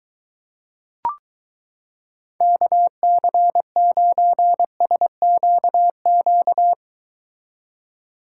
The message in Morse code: KC9SQQ

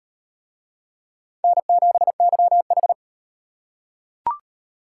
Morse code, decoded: N7YH

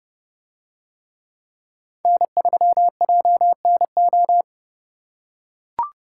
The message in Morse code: N3JNO

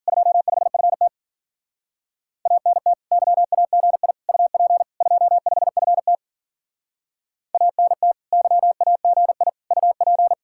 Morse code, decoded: 25FT ANT YAGI UP 25FT ANT YAGI UP

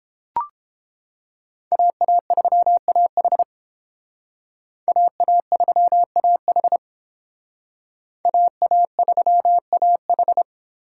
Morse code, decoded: AA3AH AA3AH AA3AH

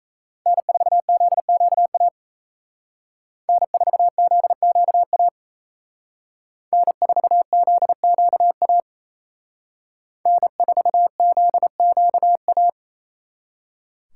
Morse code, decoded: N4ZQA N4ZQA N4ZQA N4ZQA

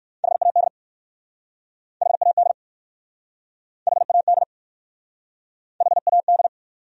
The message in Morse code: HAD HAD HAD HAD

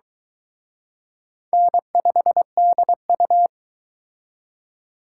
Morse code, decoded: N5DU

